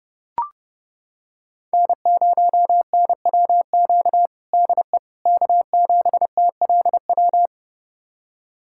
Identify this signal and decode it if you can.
Morse code, heard as a keyed tone, about 30 wpm, 700 Hz. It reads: N0NWQ DE K7TLW
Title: USA Callsign DE Callsign - 5 Characters - 30wpm